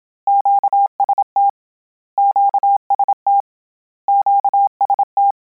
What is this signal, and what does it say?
Morse code, heard as a keyed tone, about 27 wpm, 800 Hz. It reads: QST QST QST